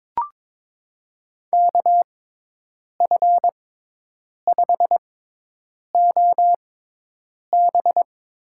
Morse code, heard as a keyed tone, about 22 wpm, spaced as for 8 wpm, 700 Hz. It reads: KF5OB
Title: USA Call Signs - 6x Character Spacing - 22wpm